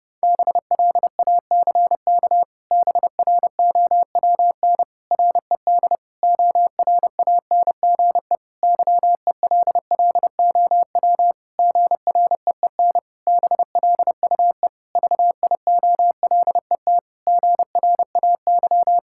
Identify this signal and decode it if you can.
Morse code, heard as a keyed tone, about 30 wpm, 700 Hz. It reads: BLACK BROWN RED ORANGE YELLOW GREEN BLUE VIOLET GRAY
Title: Pangram Challenge v2 - 30wpm